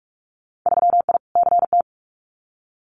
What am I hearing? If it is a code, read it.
Morse code, heard as a keyed tone, about 44 wpm, 700 Hz. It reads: 3I CT
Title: Winter Field Day Exchanges 45wpm